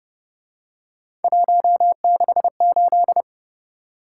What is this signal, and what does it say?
Morse code, heard as a keyed tone, about 29 wpm, 700 Hz. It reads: 168